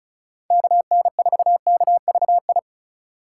Morse code, heard as a keyed tone, about 46 wpm, 700 Hz. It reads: KN4KVI